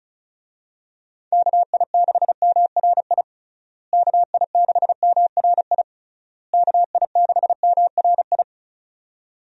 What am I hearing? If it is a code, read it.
Morse code, heard as a keyed tone, about 34 wpm, 700 Hz. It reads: KI6MRI KI6MRI KI6MRI
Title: USA Call Signs - 6 Characters - Repeated 3x - 35wpm